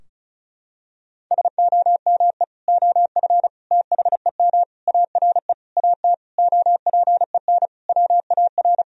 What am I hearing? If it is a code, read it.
Morse code, heard as a keyed tone, about 34 wpm, 700 Hz. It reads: SOME OF THEM ARE AT OPEN WAR
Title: Sentences from Top 1000 Words - Review Part 2 - 35wpm